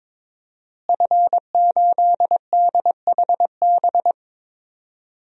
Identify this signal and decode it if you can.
Morse code, heard as a keyed tone, about 22 wpm, 700 Hz. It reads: F8DHB
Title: DXCC-100 Call Signs - 5 Characters - Repeated 3x - 22wpm